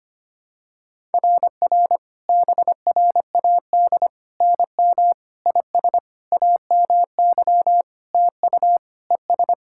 Morse code, heard as a keyed tone, about 25 wpm, 700 Hz. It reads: RR BRAD NM IS AMY TU ES
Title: SOTA QSO Practice 25wpm